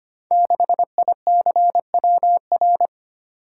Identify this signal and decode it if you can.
Morse code, heard as a keyed tone, about 25 wpm, 700 Hz. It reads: 6ICWR
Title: Code Groups - 5 Letter:Numbers - 25wpm